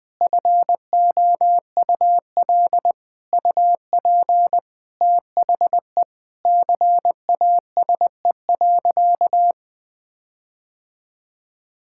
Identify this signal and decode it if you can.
Morse code, heard as a keyed tone, about 21 wpm, 700 Hz. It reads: FOUL UP THE CASE.
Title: All-Age Appropriate Jokes - 20wpm